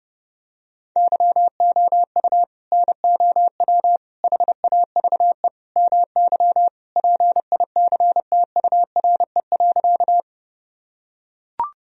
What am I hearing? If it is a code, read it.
Morse code, heard as a keyed tone, about 32 wpm, 700 Hz. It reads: YOU NOW HAVE MY PICTURE.
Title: Sentences from Top 1000 Words - Review Part 1 - 30wpm